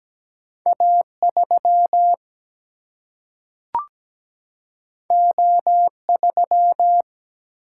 Morse code, heard as a keyed tone, about 17 wpm, 700 Hz. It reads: A 3 O 3
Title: Single-Letter Single-Number 17wpm